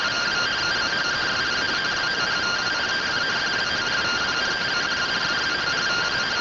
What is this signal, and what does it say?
Two-tone teleprinter signal, about 170 Hz shift.